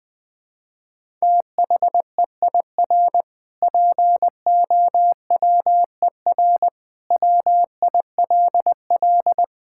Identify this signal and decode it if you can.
Morse code, heard as a keyed tone, about 20 wpm, 700 Hz. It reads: THEIR POWER WILL
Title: Sentences from Top 1000 Words - Review Part 6 - 20wpm